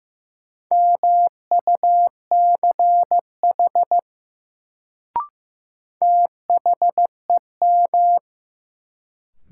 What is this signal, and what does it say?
Morse code, heard as a keyed tone, about 15 wpm, 700 Hz. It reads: MUCH THEM
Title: Sets of 2 Words - New Word Drill - 15wpm